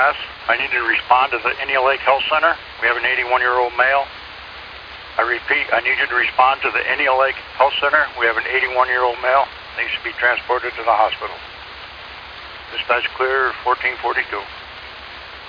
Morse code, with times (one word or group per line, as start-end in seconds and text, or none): none